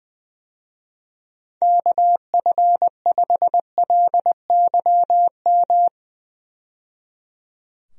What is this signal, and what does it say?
Morse code, heard as a keyed tone, about 20 wpm, 700 Hz. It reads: KF5LYM
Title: USA Call Signs - 6 Characters - 20wpm